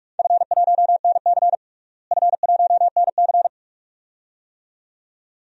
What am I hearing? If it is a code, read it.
Morse code, heard as a keyed tone, about 45 wpm, 700 Hz. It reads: F1NC F1NC
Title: DXCC-100 Call Signs - 4 Characters - Repeated 2x - 45wpm